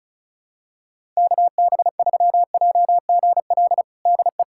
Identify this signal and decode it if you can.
Morse code, heard as a keyed tone, about 36 wpm, 700 Hz. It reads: KB3JGL DE
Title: USA Callsign DE Callsign 35wpm